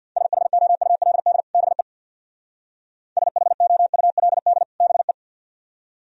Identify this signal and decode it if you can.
Morse code, heard as a keyed tone, about 61 wpm, 700 Hz. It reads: SHOULD BE SHOULD BE
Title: Speed-Racing - Sets of 2 Words 50wpm